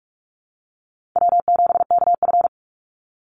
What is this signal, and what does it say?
Morse code, heard as a keyed tone, about 46 wpm, 700 Hz. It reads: W7KF
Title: Call Signs - 45wpm